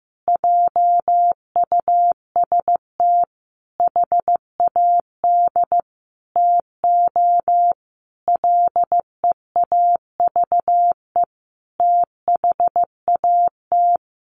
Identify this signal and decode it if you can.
Morse code, heard as a keyed tone, about 15 wpm, 700 Hz. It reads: JUST HAD TO LEAVE THAT
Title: Sentences from Top 1500 Words 15wpm